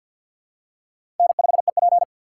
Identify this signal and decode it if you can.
Morse code, heard as a keyed tone, about 52 wpm, 700 Hz. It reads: N5EP
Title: USA Call Signs - 2x Character Spacing - 50wpm